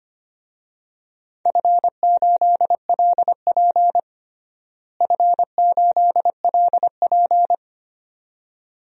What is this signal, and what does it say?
Morse code, heard as a keyed tone, about 25 wpm, 700 Hz. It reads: F8LP F8LP
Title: DXCC-100 Call Signs - 4 Characters - Repeated 2x - 25wpm